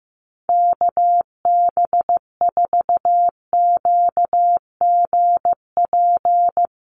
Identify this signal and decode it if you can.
Morse code, heard as a keyed tone, about 14 wpm, 700 Hz. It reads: KB4QGP
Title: USA Call Signs 15wpm